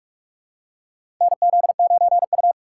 Morse code, heard as a keyed tone, about 46 wpm, 700 Hz. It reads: NZ9U
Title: USA Call Signs - 3x Character Spacing - 45wpm